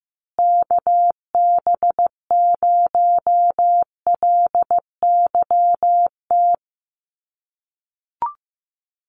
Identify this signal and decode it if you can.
Morse code, heard as a keyed tone, about 15 wpm, 700 Hz. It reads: KB0LYT
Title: USA Call Signs Repeated 3x 15wpm